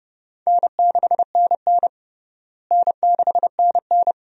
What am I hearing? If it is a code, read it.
Morse code, heard as a keyed tone, about 30 wpm, 700 Hz. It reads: N6NN N6NN
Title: USA Call Signs - 4 Characters - Repeated 2x - 30wpm